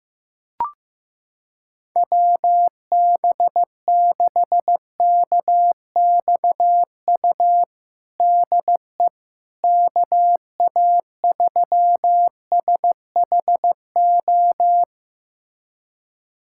Morse code, heard as a keyed tone, about 15 wpm, 700 Hz. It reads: WB6KXU DE KA3SHO
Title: USA Callsign DE Callsign 15wpm